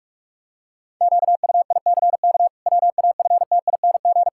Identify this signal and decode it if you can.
Morse code, heard as a keyed tone, about 46 wpm, 700 Hz. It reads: QUICK WAFTING
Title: Pangram Challenge v2 - 45wpm